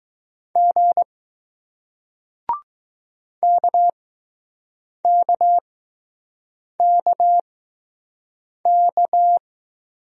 Morse code, heard as a keyed tone, about 20 wpm, spaced as for 6 wpm, 700 Hz. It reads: G KKKK